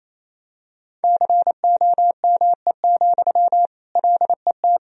Morse code, heard as a keyed tone, about 28 wpm, 700 Hz. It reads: COME, LET
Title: Sentences from Top 500 Words and Abbreviations 28wpm